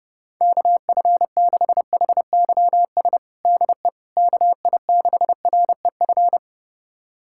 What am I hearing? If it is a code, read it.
Morse code, heard as a keyed tone, about 32 wpm, 700 Hz. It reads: KF6HYS DE KI6REF
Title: USA Callsign DE Callsign - 6 Characters - 30wpm